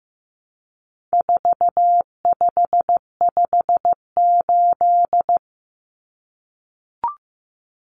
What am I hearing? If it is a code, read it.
Morse code, heard as a keyed tone, about 15 wpm, 700 Hz. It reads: 4558
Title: Numbers - 4-Digit 15wpm